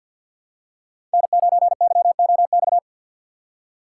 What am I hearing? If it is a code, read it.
Morse code, heard as a keyed tone, about 52 wpm, 700 Hz. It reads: N9YOX
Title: USA Call Signs - 5 Characters - 2x Character Spacing - 50wpm